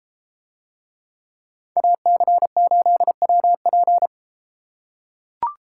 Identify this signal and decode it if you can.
Morse code, heard as a keyed tone, about 31 wpm, 700 Hz. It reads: AC8WP